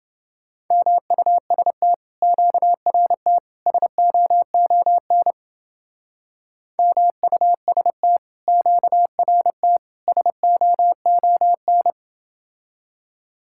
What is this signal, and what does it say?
Morse code, heard as a keyed tone, about 29 wpm, 700 Hz. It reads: MUST QRT SOON MUST QRT SOON